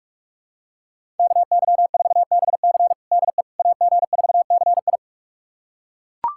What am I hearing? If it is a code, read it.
Morse code, heard as a keyed tone, about 41 wpm, 700 Hz. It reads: KY4BC DE AG4CI
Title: USA Callsign DE Callsign - 5 Characters - 45wpm